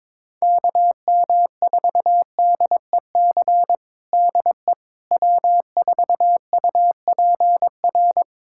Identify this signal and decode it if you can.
Morse code, heard as a keyed tone, about 22 wpm, 700 Hz. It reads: KM4DEC DE W4UPR
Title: USA Callsign DE Callsign 22wpm